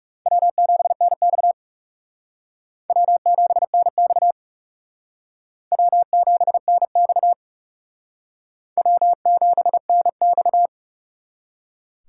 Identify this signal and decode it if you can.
Morse code, heard as a keyed tone, about 37 wpm, 700 Hz. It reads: W7NX W7NX W7NX W7NX
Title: Speed Racing - USA 1x2 Call Signs 30wpm